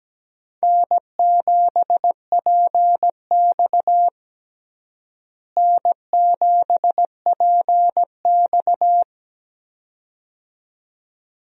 Morse code, heard as a keyed tone, about 17 wpm, 700 Hz. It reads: N7PX N7PX
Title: USA Call Signs - 4 Characters - Repeated 2x - 17wpm